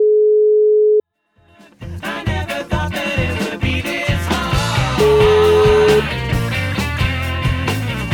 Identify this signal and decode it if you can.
Ringback tone (Europe, 425 Hz), with noise.